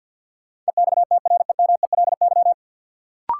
Morse code, heard as a keyed tone, about 53 wpm, 700 Hz. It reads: EXTREMELY
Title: Sentences from Top 1000 Words - Review Part 7 - New Word Drill - 50wpm